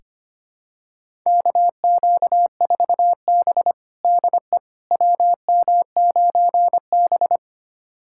Morse code, heard as a keyed tone, about 25 wpm, 700 Hz. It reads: KQ4B DE WM9B
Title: USA Callsign DE Callsign - 4 Characters - 25wpm